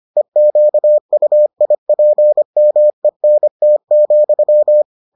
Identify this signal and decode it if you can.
Morse code, heard as a keyed tone, about 25 wpm, 600 Hz. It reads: EQUIPMENT,